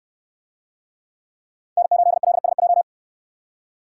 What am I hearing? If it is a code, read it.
Morse code, heard as a keyed tone, about 64 wpm, 700 Hz. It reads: N8LSJ